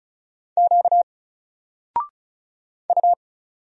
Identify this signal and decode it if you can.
Morse code, heard as a keyed tone, about 34 wpm, 700 Hz. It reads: Q U
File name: Single-Letter-Number-35wpm